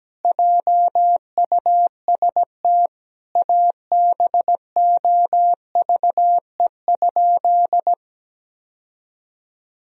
Morse code, heard as a keyed tone, about 17 wpm, 700 Hz. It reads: JUST ABOVE?
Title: Sentences from Top 1000 Words - Review Part 2 - 17wpm